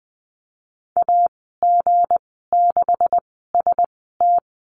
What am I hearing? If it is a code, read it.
Morse code, heard as a keyed tone, about 20 wpm, spaced as for 15 wpm, 700 Hz. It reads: AG6ST
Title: 173 Lesson 28g - Number 9 - Call Signs 20wpm